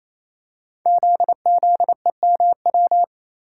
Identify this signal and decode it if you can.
Morse code, heard as a keyed tone, about 27 wpm, 700 Hz. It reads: ZZEMW
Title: Code Groups - 5 Letters - 28wpm